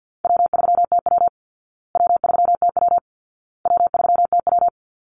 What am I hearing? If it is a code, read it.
Morse code, heard as a keyed tone, about 50 wpm, 700 Hz. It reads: W3TW W3TW W3TW
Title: Easy Call Signs Repeated 3x - 50wpm